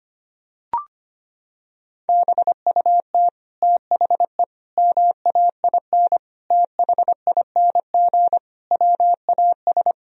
Morse code, heard as a keyed tone, about 25 wpm, 700 Hz. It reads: BUT THE MAIN THING WAS
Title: Sentences from Top 1000 Words - Review Part 2 - 25wpm